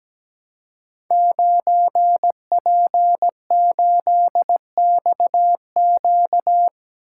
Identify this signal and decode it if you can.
Morse code, heard as a keyed tone, about 17 wpm, 700 Hz. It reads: 9P8XQ
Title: Code Groups - 5 Letter:Numbers - 17wpm